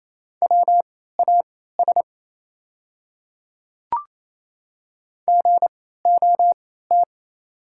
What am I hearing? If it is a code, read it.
Morse code, heard as a keyed tone, about 28 wpm, spaced as for 16 wpm, 690 Hz. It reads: WAS GOT
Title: 3x Character Spacing - Top 500 Words - 28wpm